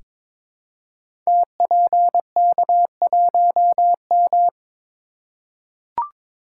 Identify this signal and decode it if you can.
Morse code, heard as a keyed tone, about 22 wpm, 700 Hz. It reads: TPK1M
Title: Code Groups - 5 Letter:Numbers - 22wpm